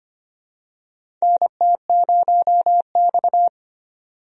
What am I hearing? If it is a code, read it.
Morse code, heard as a keyed tone, about 25 wpm, 700 Hz. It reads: NT0X
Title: USA Call Signs - 4 Characters - 25wpm